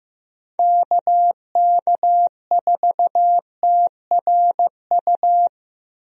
Morse code, heard as a keyed tone, about 16 wpm, 700 Hz. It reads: KK4TRU
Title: USA Call Signs - 6 Characters - Repeated 3x - 15wpm